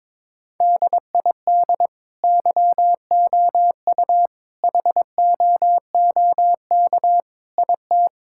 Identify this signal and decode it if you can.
Morse code, heard as a keyed tone, about 22 wpm, 700 Hz. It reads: DID YOU HOOK IT
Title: Sentences from Top 1600 Words 22wpm